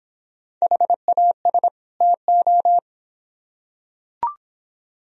Morse code, heard as a keyed tone, about 25 wpm, 700 Hz. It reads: HAS TO